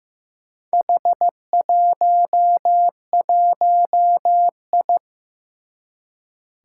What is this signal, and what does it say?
Morse code, heard as a keyed tone, about 15 wpm, 700 Hz. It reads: H11I
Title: Four Challenging Letter-Number Combinations 15wpm